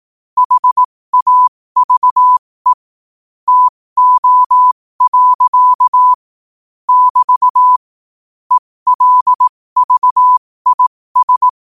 Morse code, heard as a keyed tone, about 18 wpm, spaced as for 15 wpm, 1000 Hz. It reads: HAVE TO. <BT> ELVIS
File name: qotdfiles / 15wpm